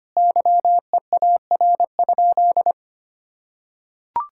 Morse code, heard as a keyed tone, about 25 wpm, 700 Hz. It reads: YEAR?
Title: Sentences from Top 1000 Words - Review Part 5 - 25wpm